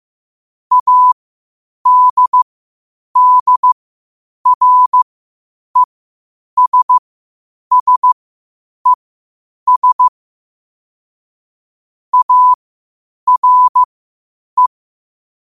Morse code, heard as a keyed tone, about 15 wpm, spaced as for 8 wpm, 1000 Hz. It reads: ADDRESSES ARE
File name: qotdfiles / 8wpm